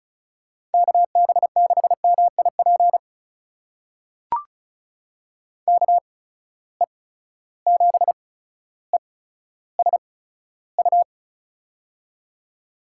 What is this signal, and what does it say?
Morse code, heard as a keyed tone, about 36 wpm, spaced as for 15 wpm, 700 Hz. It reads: KB6MIP KE7ESU
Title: USA Call Signs - 6 Characters - 8x Character Spacing - 35wpm